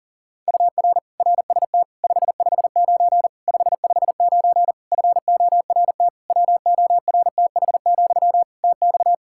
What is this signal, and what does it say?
Morse code, heard as a keyed tone, about 37 wpm, 700 Hz. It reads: UR RST 559 559 FORT WORTH, TX
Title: Practice QSO - Exchange 1 - Beginning - 40wpm